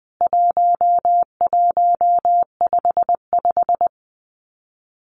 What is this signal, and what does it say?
Morse code, heard as a keyed tone, about 20 wpm, 700 Hz. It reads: 1155
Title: CWT Exchange 20wpm